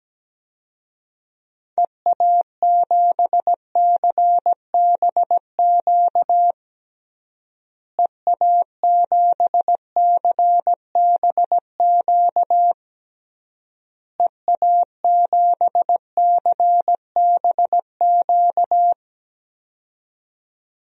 Morse code, heard as a keyed tone, about 17 wpm, 700 Hz. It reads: EA7CBQ EA7CBQ EA7CBQ
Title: DXCC-100 Call Signs - 6 Characters - Repeated 3x - 17wpm